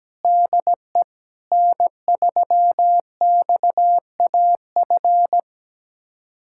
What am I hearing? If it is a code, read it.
Morse code, heard as a keyed tone, about 17 wpm, 700 Hz. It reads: DE N3XAF
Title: USA Callsign DE Callsign 17wpm